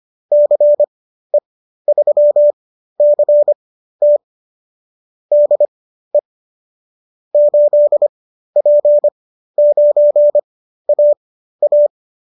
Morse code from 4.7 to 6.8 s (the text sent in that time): DE